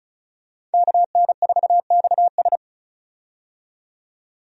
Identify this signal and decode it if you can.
Morse code, heard as a keyed tone, about 36 wpm, 700 Hz. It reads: KN4XS
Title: USA Call Signs - 5 Characters - 35wpm